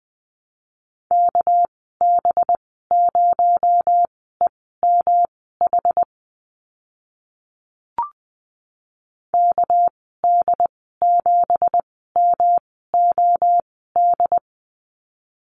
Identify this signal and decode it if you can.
Morse code, heard as a keyed tone, about 20 wpm, spaced as for 15 wpm, 700 Hz. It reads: KB0EMH KD7MOD
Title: 219 Lesson 35g - Number 0 - Call Signs 20wpm